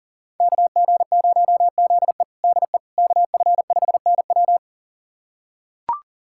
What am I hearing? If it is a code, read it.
Morse code, heard as a keyed tone, about 40 wpm, 700 Hz. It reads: KG0ZE DE KF5NW
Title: USA Callsign DE Callsign - 5 Characters - 40wpm